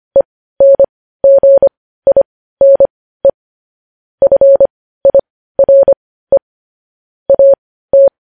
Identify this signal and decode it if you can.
Morse code, heard as a keyed tone, about 25 wpm, spaced as for 15 wpm, 550 Hz. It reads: ENGINE FIRE AT
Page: news headlines in morse code at 15 wpm